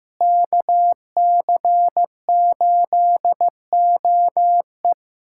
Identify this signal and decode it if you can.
Morse code, heard as a keyed tone, about 15 wpm, 700 Hz. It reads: KC8OE
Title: USA Callsign DE Callsign - 5 Characters - 15wpm